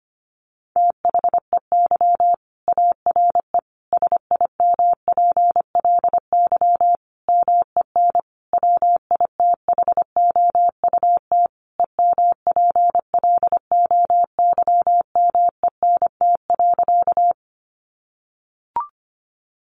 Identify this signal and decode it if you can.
Morse code, heard as a keyed tone, about 25 wpm, 700 Hz. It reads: THEY ARE SIMPLY MEN WITHOUT EMPLOYMENT.
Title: Sentences from Top 1500 Words 25wpm